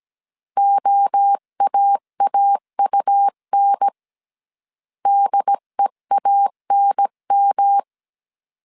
Morse code, heard as a keyed tone, about 17 wpm, 800 Hz. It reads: OAAUN DEANM
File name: CW / DET / 1 3 1500C S A 16wpm 2141